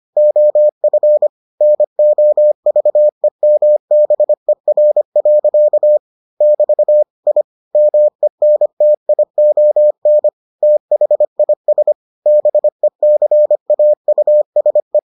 Morse code, heard as a keyed tone, about 25 wpm, 600 Hz. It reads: OF NOVEMBER. <BT> I MENTION THIS BECAUSE